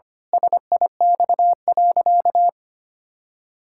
Morse code, heard as a keyed tone, about 25 wpm, 700 Hz. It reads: SIX.
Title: Pangram Challenge v2 - 25wpm